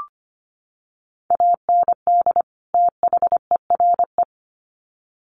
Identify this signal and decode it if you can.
Morse code, heard as a keyed tone, about 25 wpm, 700 Hz. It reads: AND THERE
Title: Sets of 2 Words 25wpm